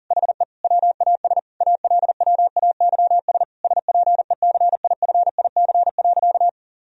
Morse code, heard as a keyed tone, about 40 wpm, 700 Hz. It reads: HE WAS ALWAYS SPECIFIC.
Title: Sentences from Top 1000 Words - Review Part 3 - 40wpm